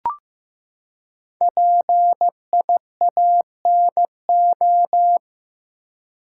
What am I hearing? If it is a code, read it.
Morse code, heard as a keyed tone, about 15 wpm, 700 Hz. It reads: PIANO
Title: Five Letter Words - 15wpm